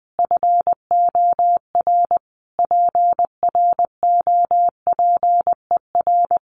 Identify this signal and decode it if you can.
Morse code, heard as a keyed tone, about 19 wpm, 700 Hz. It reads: FOR PROPER
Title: Sentences from Top 1500 Words 20wpm